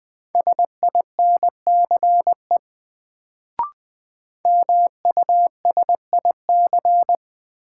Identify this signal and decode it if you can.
Morse code, heard as a keyed tone, about 20 wpm, 700 Hz. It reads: SINCE MUSIC
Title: Sentences from Top 300 Words - New Word Drill - 20wpm